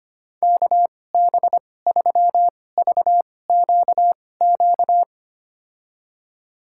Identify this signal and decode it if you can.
Morse code, heard as a keyed tone, about 25 wpm, spaced as for 18 wpm, 700 Hz. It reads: KB3VQQ
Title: USA Call Signs - 6 Characters - 2x Character Spacing - 25wpm